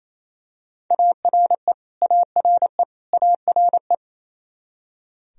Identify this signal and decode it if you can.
Morse code, heard as a keyed tone, about 28 wpm, 700 Hz. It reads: ARE ARE ARE
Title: Top 100 Words Repeated 28wpm